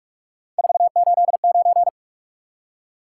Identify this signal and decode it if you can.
Morse code, heard as a keyed tone, about 45 wpm, 700 Hz. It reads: 489
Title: Top 180 QSO Elements 45wpm